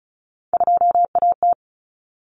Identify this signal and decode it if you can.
Morse code, heard as a keyed tone, about 34 wpm, 700 Hz. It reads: 2AT
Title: Common Cut Numbers - 3-Digit 35wpm